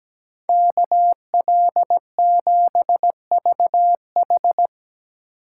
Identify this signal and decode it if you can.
Morse code, heard as a keyed tone, about 20 wpm, 700 Hz. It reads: KL7VH
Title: Speed Racing - USA 2x2 Call Signs 17wpm